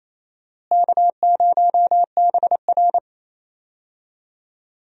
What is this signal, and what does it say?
Morse code, heard as a keyed tone, about 28 wpm, 700 Hz. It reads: K0BR
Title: USA Call Signs - 4 Characters - Repeated 2x - 28wpm